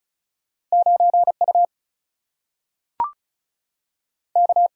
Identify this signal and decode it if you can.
Morse code, heard as a keyed tone, about 36 wpm, 700 Hz. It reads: 9 U K
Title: Single-Letter Single-Number 35wpm